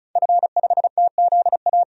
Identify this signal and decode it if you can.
Morse code, heard as a keyed tone, about 36 wpm, 700 Hz. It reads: F5TZA
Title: DXCC-100 Call Signs - 5 Characters - 35wpm